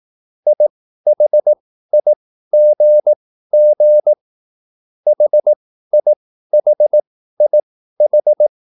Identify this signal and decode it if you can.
Morse code, heard as a keyed tone, about 18 wpm, spaced as for 13 wpm, 600 Hz. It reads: IHIGG HIHIH
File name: cwbook